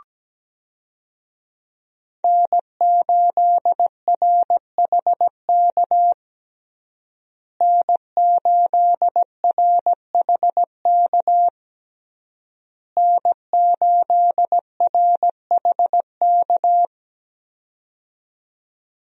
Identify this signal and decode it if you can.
Morse code, heard as a keyed tone, about 17 wpm, 700 Hz. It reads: N8RHK N8RHK N8RHK
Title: USA Call Signs - 5 Characters - Repeated 3x - 17wpm